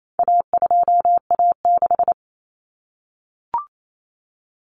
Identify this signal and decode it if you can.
Morse code, heard as a keyed tone, about 28 wpm, 700 Hz. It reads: A2A6
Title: Common Cut Numbers - 4-Digit 28wpm